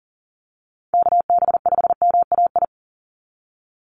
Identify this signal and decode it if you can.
Morse code, heard as a keyed tone, about 40 wpm, 700 Hz. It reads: KB5MAI
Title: USA Call Signs 40wpm